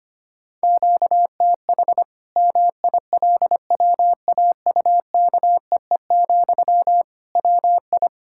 Morse code, heard as a keyed tone, about 25 wpm, 700 Hz. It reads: QTH MILWAUKEE, WI
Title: Practice QSO - Exchange 1 - Beginning - 25wpm